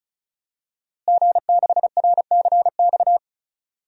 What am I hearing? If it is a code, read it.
Morse code, heard as a keyed tone, about 36 wpm, 700 Hz. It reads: G6RCX
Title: DXCC-100 Call Signs - 5 Characters - Repeated 3x - 35wpm